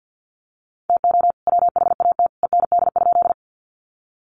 Morse code, heard as a keyed tone, about 50 wpm, 700 Hz. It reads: TO WHAT END?
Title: Sentences from Top 200 Words 50wpm